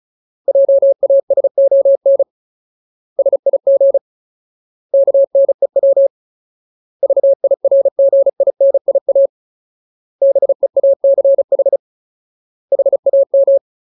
Morse code, heard as a keyed tone, about 34 wpm, 550 Hz. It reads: JASON SIG KNEW VIRGINIA BEACH HAM
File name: Ultimate Word List / 35WPM Ultimate